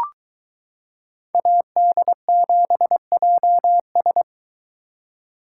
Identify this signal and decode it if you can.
Morse code, heard as a keyed tone, about 22 wpm, 700 Hz. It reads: AD7JS